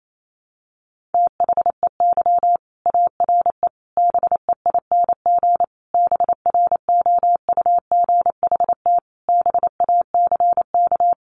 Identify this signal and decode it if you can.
Morse code, heard as a keyed tone, about 28 wpm, 700 Hz. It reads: THEY ARE BEING BROUGHT BACK
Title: Sentences from Top 600 Words 28wpm